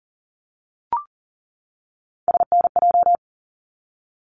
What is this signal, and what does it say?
Morse code, heard as a keyed tone, about 40 wpm, 700 Hz. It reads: SNJ
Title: ARRL Sections 40wpm